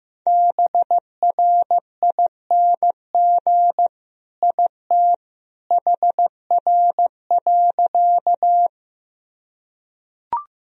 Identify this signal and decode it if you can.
Morse code, heard as a keyed tone, about 15 wpm, 700 Hz. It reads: BRING IT HR.
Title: Sentences from Top 500 Words and Abbreviations 15wpm